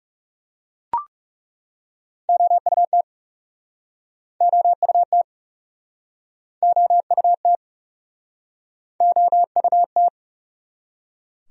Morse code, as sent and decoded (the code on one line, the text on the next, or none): --- ..- - / --- ..- - / --- ..- - / --- ..- -
OUT OUT OUT OUT